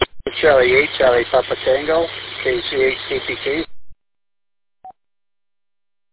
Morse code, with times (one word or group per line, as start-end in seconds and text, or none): none